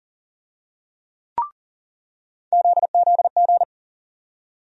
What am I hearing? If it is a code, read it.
Morse code, heard as a keyed tone, about 40 wpm, 700 Hz. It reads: ZZG
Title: USA Call Sign Postfixes - 3 Letters 40wpm